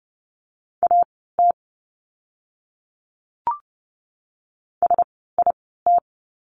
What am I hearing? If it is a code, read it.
Morse code, heard as a keyed tone, about 30 wpm, spaced as for 17 wpm, 700 Hz. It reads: AT SIT
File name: course / 30:10wpm / 115 Lesson 19f - Letter w - Words 30wpm